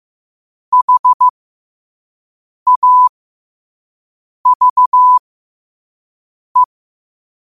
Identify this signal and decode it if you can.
Morse code, heard as a keyed tone, about 15 wpm, spaced as for 5 wpm, 1000 Hz. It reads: HAVE
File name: qotdfiles / 5wpm